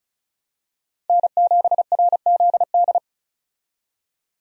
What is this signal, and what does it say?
Morse code, heard as a keyed tone, about 36 wpm, 700 Hz. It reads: N7RZD
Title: USA Call Signs - 5 Characters - 35wpm